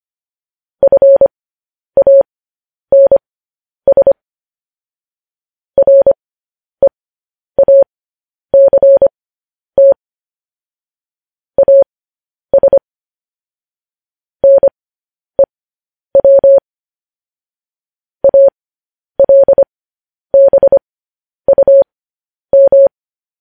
Morse code, heard as a keyed tone, about 25 wpm, spaced as for 10 wpm, 550 Hz. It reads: FANS REACT AS NEW ALBUM
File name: podcast / News headlines-1x-25x10